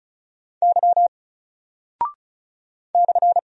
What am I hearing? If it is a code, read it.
Morse code, heard as a keyed tone, about 36 wpm, 700 Hz. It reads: Y /
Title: Single Character (Letter:Number:Punctuation:Prosign) Alt 35wpm